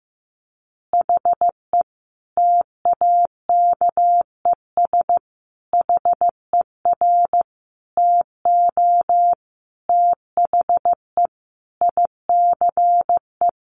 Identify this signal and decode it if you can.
Morse code, heard as a keyed tone, about 14 wpm, 700 Hz. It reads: HE TAKES HER TO THE ICE
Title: Additional Easy Phrases 15wpm